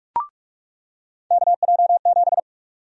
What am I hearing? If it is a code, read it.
Morse code, heard as a keyed tone, about 46 wpm, 700 Hz. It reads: KJ7
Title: USA Call Sign Prefixes - 45wpm